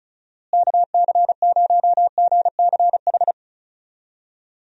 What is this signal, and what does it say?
Morse code, heard as a keyed tone, about 34 wpm, 700 Hz. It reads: KC0GCH
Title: USA Call Signs - 6 Characters - Repeated 3x - 35wpm